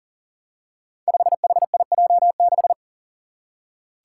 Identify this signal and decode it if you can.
Morse code, heard as a keyed tone, about 40 wpm, 700 Hz. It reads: 5HIJ6
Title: Five Challenging Letter-Number Combinations 40wpm